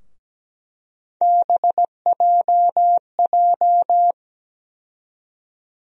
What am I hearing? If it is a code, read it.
Morse code, heard as a keyed tone, about 17 wpm, 700 Hz. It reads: BJJ
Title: USA Call Sign Postfixes - 3 Letters 17wpm